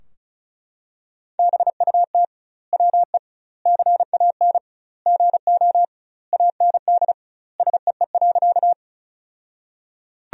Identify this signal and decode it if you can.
Morse code, heard as a keyed tone, about 34 wpm, 700 Hz. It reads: BUT WE CAN GO AND SEE.
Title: 2x Word-Spacing - Sentences from Top 100 Words - 35wpm